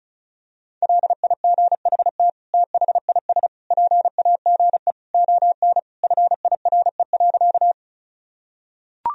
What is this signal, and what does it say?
Morse code, heard as a keyed tone, about 34 wpm, 700 Hz. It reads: LIGHT THIS PAGE ON FIRE.
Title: Sentences from Top 1000 Words - Review Part 1 - 35wpm